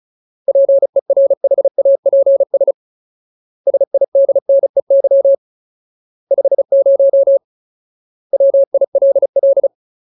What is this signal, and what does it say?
Morse code, heard as a keyed tone, about 35 wpm, 550 Hz. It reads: PERHAPS SIDNEY 50 WILL
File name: Ultimate Word List / 35WPM Ultimate